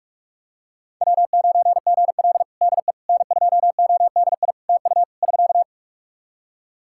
Morse code, heard as a keyed tone, about 43 wpm, 700 Hz. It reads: W9GL DE N2ODI TU <SK>
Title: Practice QSO - Exchange 4 - Ending - 45wpm